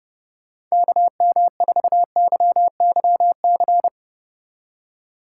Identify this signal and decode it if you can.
Morse code, heard as a keyed tone, about 37 wpm, 700 Hz. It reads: KM4YYC
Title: Speed Racing - USA Call Signs 30wpm